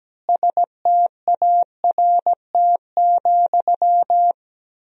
Morse code, heard as a keyed tone, about 17 wpm, 700 Hz. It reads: START,
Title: 2x Word-Spacing - Sentences from Top 300 Words - 17wpm